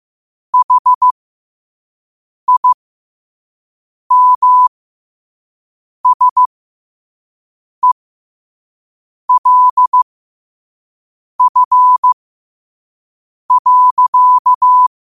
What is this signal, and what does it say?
Morse code, heard as a keyed tone, about 15 wpm, spaced as for 5 wpm, 1000 Hz. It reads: HIMSELF.
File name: qotdfiles / 5wpm